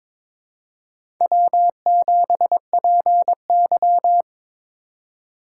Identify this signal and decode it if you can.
Morse code, heard as a keyed tone, about 22 wpm, 700 Hz. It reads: W7PY
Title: USA Call Signs - 4 Characters - 2x Character Spacing - 22wpm